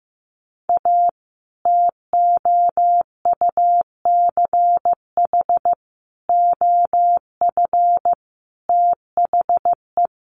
Morse code, heard as a keyed tone, about 15 wpm, 700 Hz. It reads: A TOUCH OF THE
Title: A Sound Of Thunder 15wpm